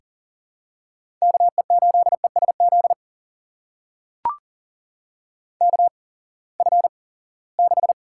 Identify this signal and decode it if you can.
Morse code, heard as a keyed tone, about 40 wpm, spaced as for 23 wpm, 700 Hz. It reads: KE8ESZ KF6
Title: USA Call Signs - 6 Characters - 8x Character Spacing - 40wpm